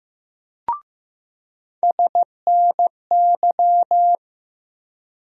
Morse code, heard as a keyed tone, about 15 wpm, 700 Hz. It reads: SNY
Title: USA Call Sign Postfixes - 3 Letters 15wpm